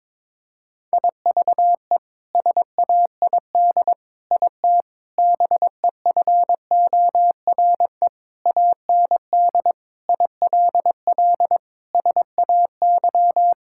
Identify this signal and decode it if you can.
Morse code, heard as a keyed tone, about 21 wpm, 700 Hz. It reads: IVE SAID IT BEFORE AND ILL SAY
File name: practice / 22wpm / Sets of 8 Words - 22wpm